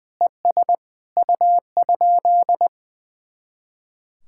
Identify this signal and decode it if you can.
Morse code, heard as a keyed tone, about 20 wpm, 700 Hz. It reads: ES U?
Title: Speed-Racing Common QSO Phrases Encore - 20wpm